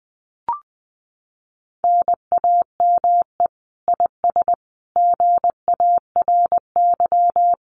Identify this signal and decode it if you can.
Morse code, heard as a keyed tone, about 20 wpm, 700 Hz. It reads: NAME IS GARY
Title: Common QSO Phrases Encore 20wpm